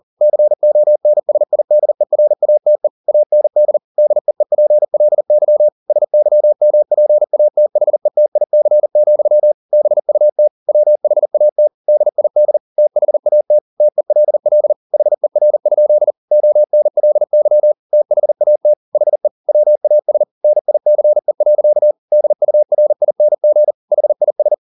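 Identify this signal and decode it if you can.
Morse code, heard as a keyed tone, about 40 wpm, 600 Hz. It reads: CONSIDERATE AND DEEPLY SYMPATHETIC, BUT WHAT DID THAT TELL HER? ONLY THAT HE WAS NICE. DURING HIS
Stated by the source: Morse code at 40 wpm, practice text, CW med Gnister.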